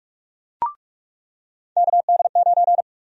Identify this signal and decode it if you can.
Morse code, heard as a keyed tone, about 46 wpm, 700 Hz. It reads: KD9
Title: USA Call Sign Prefixes - 45wpm